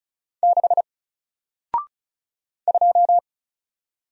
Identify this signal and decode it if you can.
Morse code, heard as a keyed tone, about 36 wpm, 700 Hz. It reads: 6 2
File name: Single-Letter-Number-35wpm